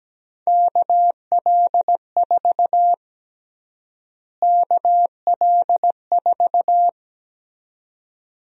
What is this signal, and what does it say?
Morse code, heard as a keyed tone, about 17 wpm, 700 Hz. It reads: KL4 KL4
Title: USA Call Sign Prefixes - Repeated 2x - 17wpm